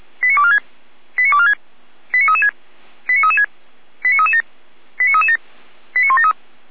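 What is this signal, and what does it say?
Five-tone selective call (ZVEI), heard as quick runs of five tones, tones of about 70 ms. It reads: 89246 89246 89336 89336 89337 89337 89173
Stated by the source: ZVEI five-tone selective call received by SDR, logged by Feuerwehr Orlamünde (fire brigade).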